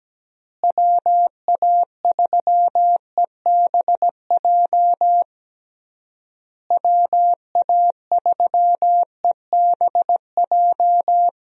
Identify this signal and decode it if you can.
Morse code, heard as a keyed tone, about 17 wpm, 700 Hz. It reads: WA3EBJ WA3EBJ
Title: USA Call Signs - 6 Characters - Repeated 3x - 17wpm